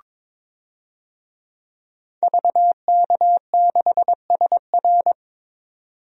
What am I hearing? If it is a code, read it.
Morse code, heard as a keyed tone, about 22 wpm, 700 Hz. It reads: VK6SR
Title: DXCC-100 Call Signs - 5 Characters - 22wpm